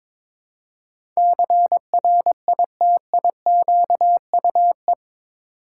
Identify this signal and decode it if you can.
Morse code, heard as a keyed tone, about 22 wpm, 700 Hz. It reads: CRITIQUE
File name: practice / 22wpm / Eight Letter Words - 22wpm